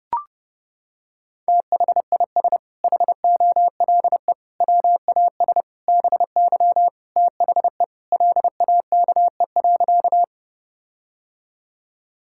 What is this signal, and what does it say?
Morse code, heard as a keyed tone, about 32 wpm, 700 Hz. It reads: THIS HOLE WAS BY THE LAKE.
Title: Sentences from Top 1600 Words 30wpm